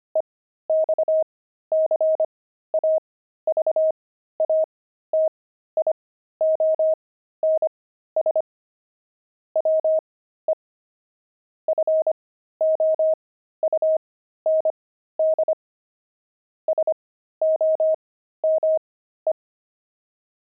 Morse code, heard as a keyed tone, about 25 wpm, spaced as for 13 wpm, 630 Hz. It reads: EXCAVATIONS WE FOUND SOME
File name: pangram-fs-13-cs-25-deep